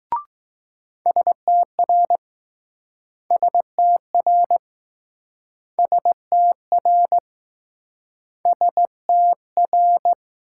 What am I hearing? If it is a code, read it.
Morse code, heard as a keyed tone, about 20 wpm, 700 Hz. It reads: STR STR STR STR